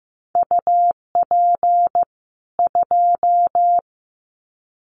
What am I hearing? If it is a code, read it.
Morse code, heard as a keyed tone, about 15 wpm, 700 Hz. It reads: UP 2
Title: Common QSO Phrases 15wpm